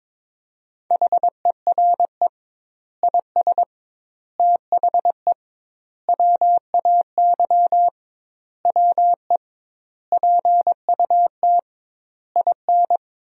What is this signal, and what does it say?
Morse code, heard as a keyed tone, about 21 wpm, 700 Hz. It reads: HERE IS THE WAY WE PUT IN
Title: 2x Word-Spacing - Sentences from Top 200 Words - 22wpm